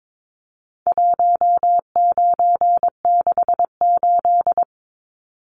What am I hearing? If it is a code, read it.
Morse code, heard as a keyed tone, about 22 wpm, 700 Hz. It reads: 1968
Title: Numbers - 4-Digit 22wpm